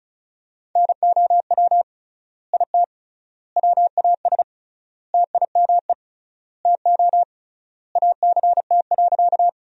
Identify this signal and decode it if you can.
Morse code, heard as a keyed tone, about 36 wpm, 700 Hz. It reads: NOW IT WAS TIME TO ACT.
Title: 3x Word-Spacing - Sentences from Top 200 Words - 35wpm